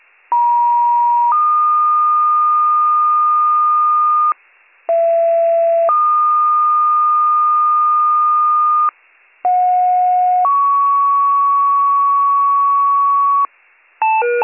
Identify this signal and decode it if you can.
Two-tone page, heard as a one-second tone, then a three-second tone, about 950 then 1220 Hz, about 670 then 1220 Hz, about 730 then 1120 Hz. It is heard from 0.3 to 13.5 s.